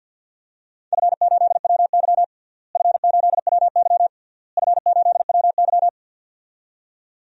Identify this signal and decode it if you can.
Morse code, heard as a keyed tone, about 51 wpm, 700 Hz. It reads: F8WY F8WY F8WY
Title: DXCC-100 Call Signs - 4 Characters - Repeated 3x - 50wpm